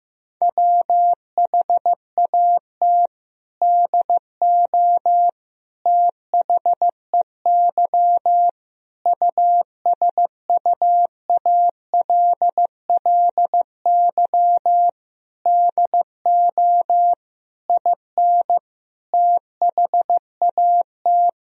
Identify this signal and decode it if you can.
Morse code, heard as a keyed tone, about 16 wpm, 700 Hz. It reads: WHAT DO THEY USUALLY DO IN THAT
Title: Sentences from Top 1000 Words - Review Part 3 - 15wpm